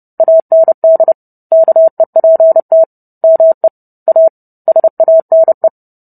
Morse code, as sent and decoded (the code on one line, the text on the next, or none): .- -. -.. / -.- . .--. - / -- . / .- / ... .- -. .
AND KEPT ME A SANE